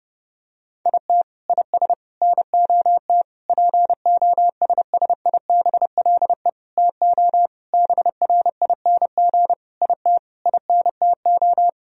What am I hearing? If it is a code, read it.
Morse code, heard as a keyed tone, about 30 wpm, 700 Hz. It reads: IT IS NOT POSSIBLE TO BRING IT INTO
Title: Sentences from Top 1000 Words - Review Part 4 - 30wpm